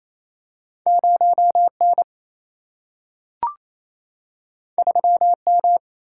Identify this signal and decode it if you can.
Morse code, heard as a keyed tone, about 28 wpm, 700 Hz. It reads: 0N 3M
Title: Single-Letter Single-Number 28wpm